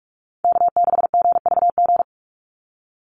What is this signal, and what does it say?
Morse code, heard as a keyed tone, about 44 wpm, 700 Hz. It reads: K6GVG
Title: Call Signs Repeated 3x - 45wpm